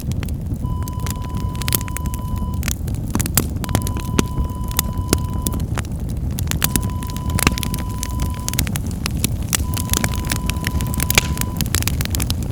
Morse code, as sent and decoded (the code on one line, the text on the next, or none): none